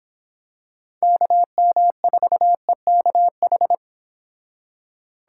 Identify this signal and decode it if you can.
Morse code, heard as a keyed tone, about 25 wpm, 700 Hz. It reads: KM4EKH